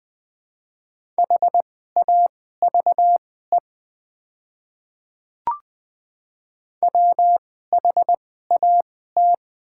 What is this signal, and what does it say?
Morse code, heard as a keyed tone, about 20 wpm, spaced as for 14 wpm, 700 Hz. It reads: HAVE WHAT
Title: 2x Character Spacing - Top 200 Words - 20wpm